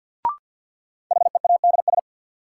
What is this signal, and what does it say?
Morse code, heard as a keyed tone, about 47 wpm, 700 Hz. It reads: HEADS
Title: Sentences from Top 1000 Words - Review Part 3 - New Word Drill - 50wpm